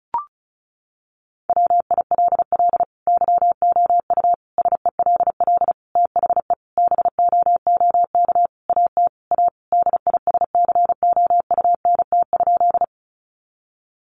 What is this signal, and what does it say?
Morse code, heard as a keyed tone, about 34 wpm, 700 Hz. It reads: WILL YOU SELL THE BOOK AT A DISCOUNT?
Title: Sentences from Top 1200 Words 35wpm